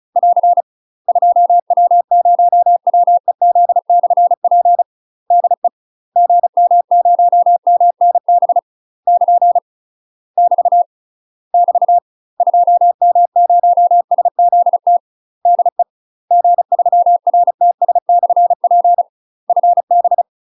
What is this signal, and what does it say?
Morse code, heard as a keyed tone, about 34 wpm, 700 Hz. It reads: <AR> 2W0WEZ/P DE GM0MNB <KN> <BT> <BT> 2M0SZT DE G3RTS/P FB
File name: UK MASTER QSOs TRAINING / 35WPM / UK MASTER QSOs 0003